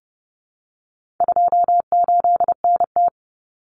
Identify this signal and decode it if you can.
Morse code, heard as a keyed tone, about 30 wpm, 700 Hz. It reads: 28NT
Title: Common Cut Numbers - 4-Digit 30wpm